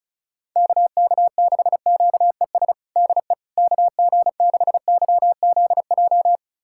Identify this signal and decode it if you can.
Morse code, heard as a keyed tone, about 36 wpm, 700 Hz. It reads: KK6QES DE KG6YZJ
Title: USA Callsign DE Callsign - 6 Characters - 35wpm